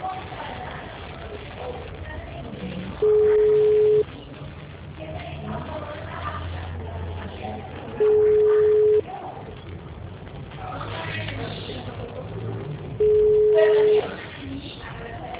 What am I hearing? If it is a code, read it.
Ringback tone (Europe, 425 Hz), with speech.